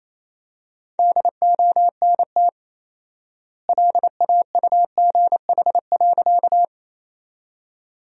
Morse code, heard as a keyed tone, about 27 wpm, 700 Hz. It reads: DONT LAUGH.
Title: 4x Word-Spacing - Sentences from Top 500 Words - 28wpm